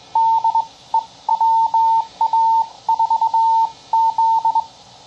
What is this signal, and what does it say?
Morse code, heard as a keyed tone, about 22 wpm, spaced as for 18 wpm, 900 Hz. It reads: DE WA4Z
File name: CW Learning Programs / sounds and personalities of cw and transmitters / 15 - Bug